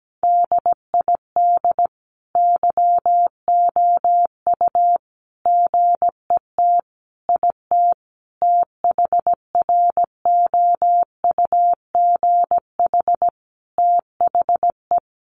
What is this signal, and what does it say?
Morse code, heard as a keyed tone, about 17 wpm, 700 Hz. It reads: DID YOU GET IT THROUGH THE
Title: Sentences from Top 1300 Words 17wpm